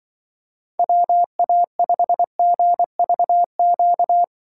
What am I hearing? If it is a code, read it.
Morse code, heard as a keyed tone, about 24 wpm, 700 Hz. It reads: WA5GVQ
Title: Speed Racing - USA 2x3 Call Signs 20wpm